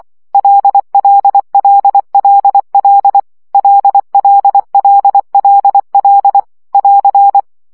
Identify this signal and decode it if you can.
Morse code, heard as a keyed tone, about 24 wpm, 800 Hz. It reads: LLLLL LLLLL <AR>
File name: CW gyakorlas l 20 baud